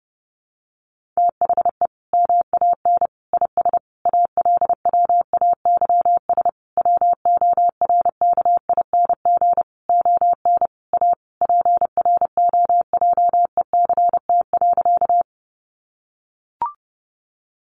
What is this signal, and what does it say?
Morse code, heard as a keyed tone, about 30 wpm, 700 Hz. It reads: THE MAN IS ALWAYS WORKING ON A PROJECT.
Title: Sentences from Top 1400 Words 30wpm